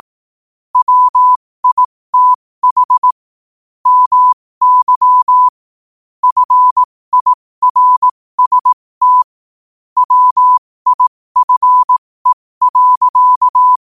Morse code, heard as a keyed tone, about 18 wpm, spaced as for 15 wpm, 1000 Hz. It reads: WITH MY FIRST WIFE.
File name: qotdfiles / 15wpm